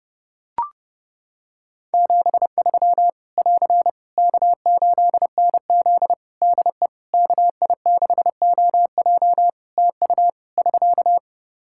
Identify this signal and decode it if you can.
Morse code, heard as a keyed tone, about 32 wpm, 700 Hz. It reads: 73 <AR> K8NZ DE KI6OJ TU <SK>
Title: Practice QSO - Exchange 4 - Ending - 30wpm